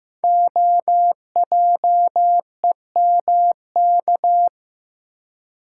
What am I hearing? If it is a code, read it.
Morse code, heard as a keyed tone, about 15 wpm, 700 Hz. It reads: OJEMK
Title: Code Groups - 5 Letters - 15wpm